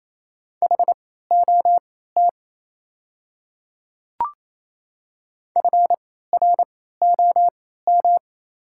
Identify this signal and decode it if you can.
Morse code, heard as a keyed tone, about 28 wpm, spaced as for 16 wpm, 700 Hz. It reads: HOT FROM
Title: 3x Character Spacing - Top 100 Words - 28wpm